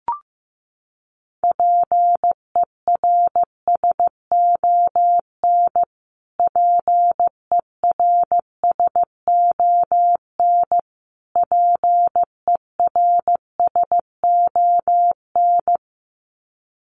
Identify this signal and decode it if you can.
Morse code, heard as a keyed tone, about 15 wpm, 700 Hz. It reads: PERSON PERSON PERSON
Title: Top 500 Words Repeated 15wpm